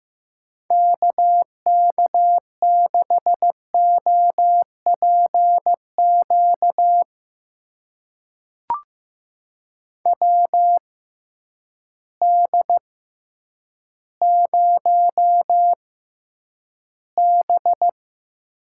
Morse code, heard as a keyed tone, about 15 wpm, spaced as for 9 wpm, 700 Hz. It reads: KK6OPQ WD0B
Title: USA Call Signs - 6x Character Spacing - 15wpm